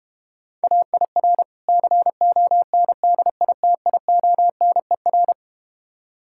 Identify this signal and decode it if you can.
Morse code, heard as a keyed tone, about 32 wpm, 700 Hz. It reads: AIR CONDITIONER